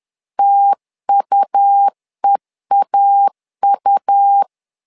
Morse code, heard as a keyed tone, about 10 wpm, 800 Hz. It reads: TUEAU
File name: LEZIONI / 2 UHDN / 2 U 10wpm 1622